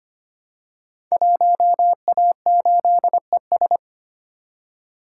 Morse code, heard as a keyed tone, about 25 wpm, 700 Hz. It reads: 1A8ES
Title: Code Groups - 5 Letter:Numbers - 25wpm